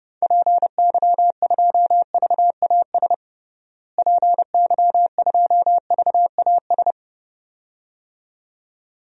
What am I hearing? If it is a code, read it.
Morse code, heard as a keyed tone, about 32 wpm, 700 Hz. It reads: PY2VAS PY2VAS
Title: DXCC-100 Call Signs - 6 Characters - Repeated 2x - 30wpm